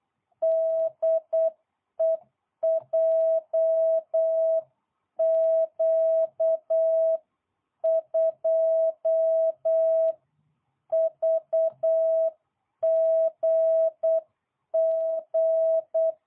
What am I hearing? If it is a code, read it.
Morse code, heard as a keyed tone, about 8 wpm, 650 Hz. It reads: DE JQ2VGG